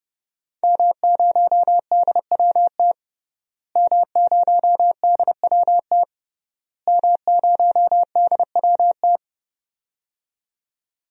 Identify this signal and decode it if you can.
Morse code, heard as a keyed tone, about 30 wpm, 700 Hz. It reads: M0DWT M0DWT M0DWT
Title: DXCC-100 Call Signs - 5 Characters - Repeated 3x - 30wpm